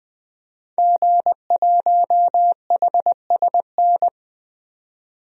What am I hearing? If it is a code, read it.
Morse code, heard as a keyed tone, about 20 wpm, 700 Hz. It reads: G1HSN
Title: DXCC-100 Call Signs - 5 Characters - 20wpm